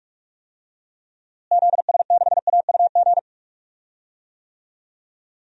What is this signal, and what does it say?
Morse code, heard as a keyed tone, about 43 wpm, 700 Hz. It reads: ZS6AUG
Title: DXCC-100 Call Signs - 6 Characters - Repeated 2x - 45wpm